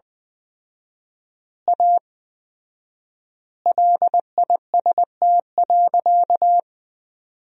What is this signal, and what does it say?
Morse code, heard as a keyed tone, about 20 wpm, 700 Hz. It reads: A LIST.
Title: 4x Word-Spacing - Sentences from Top 400 Words - 20wpm